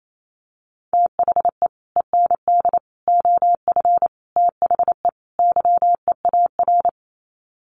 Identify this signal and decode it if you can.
Morse code, heard as a keyed tone, about 28 wpm, 700 Hz. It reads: THE END OF THE YEAR
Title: Sets of 5 Words 28wpm